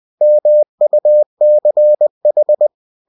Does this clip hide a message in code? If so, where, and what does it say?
Yes: Morse code from 0.2 to 2.7 s, heard as a keyed tone, about 20 wpm, 600 Hz. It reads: MUCH